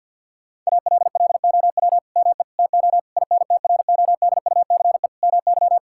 Morse code, heard as a keyed tone, about 50 wpm, 700 Hz. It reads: ALLOW ME TO INTRODUCE MY
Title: Sentences from Top 1600 Words 50wpm